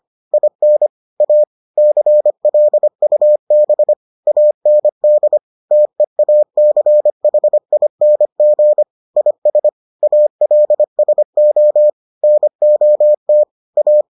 Morse code, heard as a keyed tone, about 25 wpm, 600 Hz. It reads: IN A CLUB AND TEACHING IS ALSO NOT A